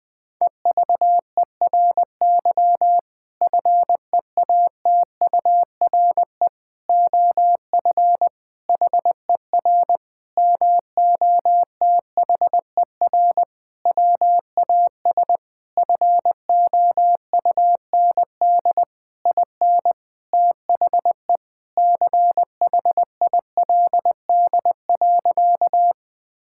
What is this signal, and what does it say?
Morse code, heard as a keyed tone, about 20 wpm, 700 Hz. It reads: EVERY FEATURE OF HER MOTHER WAS FOUND IN THE CHILD.
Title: Sentences from Top 1000 Words - Review Part 5 - 20wpm